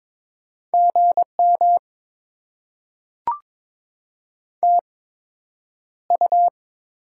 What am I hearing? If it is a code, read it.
Morse code, heard as a keyed tone, about 22 wpm, spaced as for 9 wpm, 700 Hz. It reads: GM TU
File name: Top 20 QSO Elements - 8x Character Spacing - 22wpm